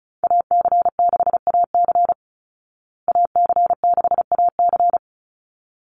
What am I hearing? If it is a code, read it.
Morse code, heard as a keyed tone, about 34 wpm, 700 Hz. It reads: AC6AC AC6AC
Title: Call Signs Repeated 3x - 35wpm